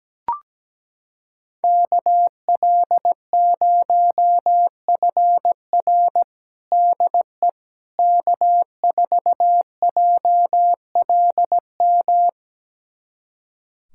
Morse code, heard as a keyed tone, about 17 wpm, 700 Hz. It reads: KL0FR DE K4JLM
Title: USA Callsign DE Callsign - 5 Characters - 17wpm